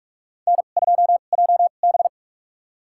Morse code, heard as a keyed tone, about 43 wpm, spaced as for 32 wpm, 700 Hz. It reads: N2JB
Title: USA Call Signs - 4 Characters - 2x Character Spacing - 45wpm